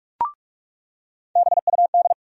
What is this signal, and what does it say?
Morse code, heard as a keyed tone, about 45 wpm, 700 Hz. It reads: BUD
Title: USA Call Sign Postfixes - 3 Letters 45wpm